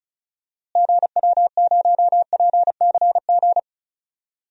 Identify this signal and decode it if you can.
Morse code, heard as a keyed tone, about 34 wpm, 700 Hz. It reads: GW0PCG
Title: DXCC-100 Call Signs - 6 Characters - Repeated 3x - 35wpm